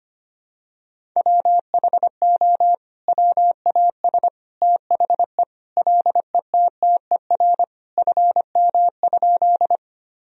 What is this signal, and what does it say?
Morse code, heard as a keyed tone, about 25 wpm, 700 Hz. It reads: WHO WAS THE LETTER FM?
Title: Sentences from Top 300 Words and Abbreviations 25wpm